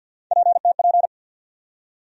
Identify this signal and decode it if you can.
Morse code, heard as a keyed tone, about 53 wpm, 700 Hz. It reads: PTP
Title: USA Call Sign Postfixes - 3 Letters 50wpm